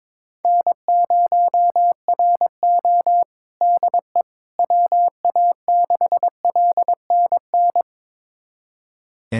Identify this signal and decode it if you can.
Morse code, heard as a keyed tone, about 22 wpm, 700 Hz. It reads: N0RO DE WA6LNN
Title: USA Callsign DE Callsign 22wpm